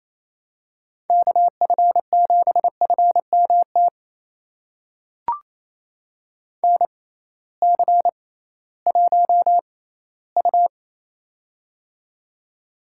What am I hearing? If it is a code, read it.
Morse code, heard as a keyed tone, about 28 wpm, spaced as for 16 wpm, 700 Hz. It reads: KF7FMT NC1U
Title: USA Call Signs - 6x Character Spacing - 28wpm